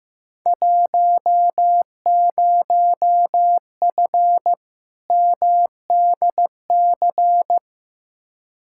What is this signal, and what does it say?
Morse code, heard as a keyed tone, about 15 wpm, 700 Hz. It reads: 10F MDC
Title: Field Day Exchanges Encore - 15wpm